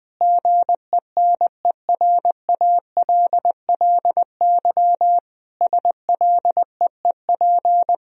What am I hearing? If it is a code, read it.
Morse code, heard as a keyed tone, about 20 wpm, 700 Hz. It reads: GENERALLY SLEEP
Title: Sentences from Top 1000 Words - Review Part 4 - 20wpm